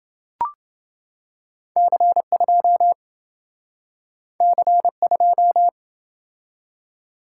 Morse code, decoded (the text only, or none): C2 C2